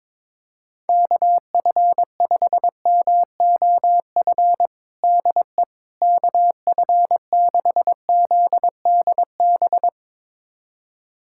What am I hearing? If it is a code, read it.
Morse code, heard as a keyed tone, about 21 wpm, 700 Hz. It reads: KF5MOF DE KF6ZDB
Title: USA Callsign DE Callsign - 6 Characters - 22wpm